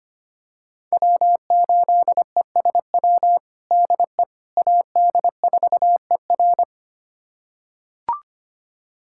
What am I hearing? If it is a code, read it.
Morse code, heard as a keyed tone, about 25 wpm, 700 Hz. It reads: W8ESW DE AD4ER
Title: USA Callsign DE Callsign - 5 Characters - 25wpm